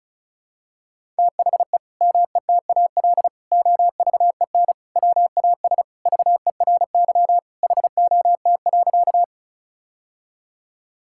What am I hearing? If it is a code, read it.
Morse code, heard as a keyed tone, about 36 wpm, 700 Hz. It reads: THE METAL OVEN WAS VERY HOT.
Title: Sentences from Top 1000 Words - Review Part 6 - 35wpm